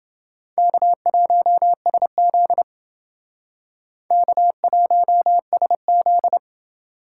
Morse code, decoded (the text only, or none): K1SZ K1SZ